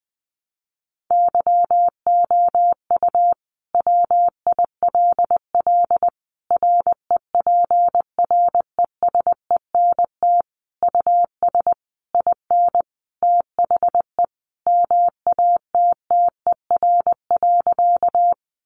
Morse code, heard as a keyed tone, about 20 wpm, 700 Hz. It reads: YOU WILL REPRESENT US IN THE MATTER.
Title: Sentences from Top 1200 Words 20wpm